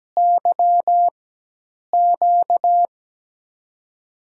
Morse code, heard as a keyed tone, about 17 wpm, spaced as for 8 wpm, 700 Hz. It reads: YQ
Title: USA Call Signs - 5 Characters - 4x Character Spacing - 17wpm